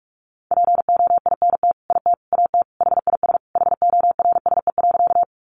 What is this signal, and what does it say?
Morse code, heard as a keyed tone, about 44 wpm, 700 Hz. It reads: POINT IT AT HIS HORSE.
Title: Sentences from Top 300 Words 45wpm